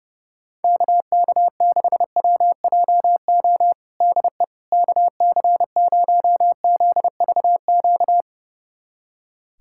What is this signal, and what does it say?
Morse code, heard as a keyed tone, about 30 wpm, 700 Hz. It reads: KK6WJO DE KC0ZVQ
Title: USA Callsign DE Callsign 30wpm